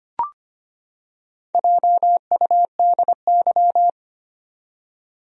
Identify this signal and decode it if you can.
Morse code, heard as a keyed tone, about 25 wpm, 700 Hz. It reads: JUDY
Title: USA Common Names - Max 4 Letters - 25wpm